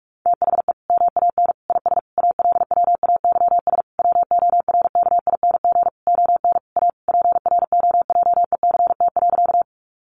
Morse code, heard as a keyed tone, about 44 wpm, 700 Hz. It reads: THE MAN IS ALWAYS WORKING ON A PROJECT.
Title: Sentences from Top 1400 Words 45wpm